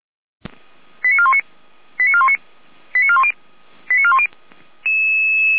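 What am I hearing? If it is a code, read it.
Five-tone selective call (ZVEI), heard as quick runs of five tones, tones of about 70 ms. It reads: 89419 89419 89410 89410
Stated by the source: ZVEI five-tone call 85374 received on SDR.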